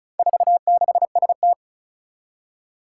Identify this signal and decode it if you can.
Morse code, heard as a keyed tone, about 36 wpm, 700 Hz. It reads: 46ST
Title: Four Challenging Letter-Number Combinations 35wpm